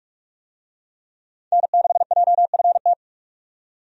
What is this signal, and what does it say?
Morse code, heard as a keyed tone, about 44 wpm, 700 Hz. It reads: N6JFT
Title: USA Call Signs - 5 Characters - 45wpm